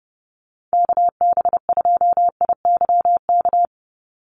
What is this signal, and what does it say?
Morse code, heard as a keyed tone, about 29 wpm, 700 Hz. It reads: KB2IYK
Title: USA Call Signs 30wpm